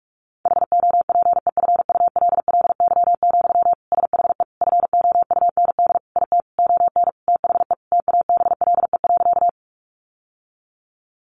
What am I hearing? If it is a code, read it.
Morse code, heard as a keyed tone, about 44 wpm, 700 Hz. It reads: HOPEFULLY, SHE FOUND IT ON THE TABLE.
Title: Sentences from Top 1500 Words 45wpm